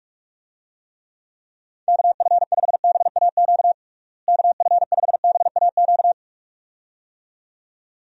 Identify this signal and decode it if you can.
Morse code, heard as a keyed tone, about 46 wpm, 700 Hz. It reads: KF5BAQ KF5BAQ
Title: USA Call Signs - 6 Characters - Repeated 2x - 45wpm